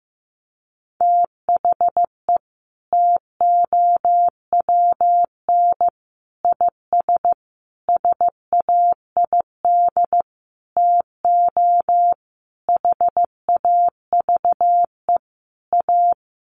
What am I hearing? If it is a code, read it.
Morse code, heard as a keyed tone, about 15 wpm, 700 Hz. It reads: THE TOWN IS SAID TO HAVE A
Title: Sentences from Top 1300 Words 15wpm